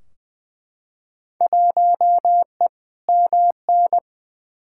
Morse code, heard as a keyed tone, about 20 wpm, 700 Hz. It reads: 1E MN
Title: Field Day Exchanges - 20wpm